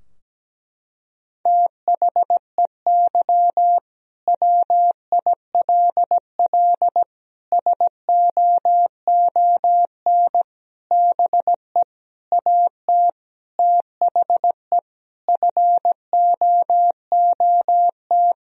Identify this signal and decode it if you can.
Morse code, heard as a keyed tone, about 17 wpm, 700 Hz. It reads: THEY WILL SOON BE AT THE FOOT
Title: Sentences from Top 1000 Words - Review Part 4 - 17wpm